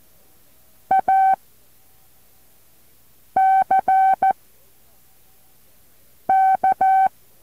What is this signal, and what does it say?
Morse code, heard as a keyed tone, about 14 wpm, 770 Hz. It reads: A C K
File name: intl morse code2